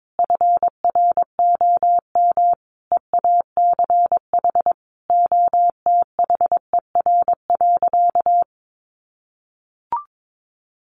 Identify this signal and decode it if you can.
Morse code, heard as a keyed tone, about 22 wpm, 700 Hz. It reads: FROM EACH OTHER.
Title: Easy Phrases 22wpm